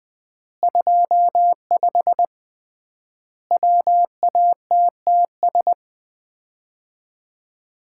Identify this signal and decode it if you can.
Morse code, heard as a keyed tone, about 20 wpm, 700 Hz. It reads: 25 WATTS
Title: Common QSO Phrases - 3x Word-Spacing - 20wpm